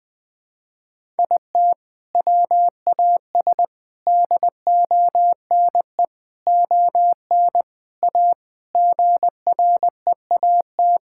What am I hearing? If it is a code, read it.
Morse code, heard as a keyed tone, about 20 wpm, 700 Hz. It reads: IT WAS DONE ON A GREAT
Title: Sentences from Top 1000 Words - Review Part 5 - 20wpm